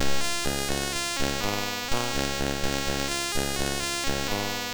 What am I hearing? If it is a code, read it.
There is no code.